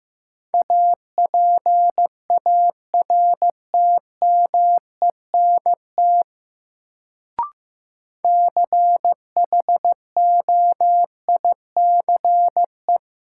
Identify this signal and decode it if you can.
Morse code, heard as a keyed tone, about 15 wpm, 700 Hz. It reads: APARTMENT CHOICE
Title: Sentences from Top 900 Words - New Word Drill - 15wpm